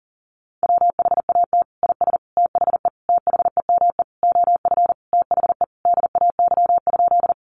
Morse code, heard as a keyed tone, about 40 wpm, 700 Hz. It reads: WHAT IS THE THEME OF THE DAY?
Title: Sentences from Top 1100 Words 40wpm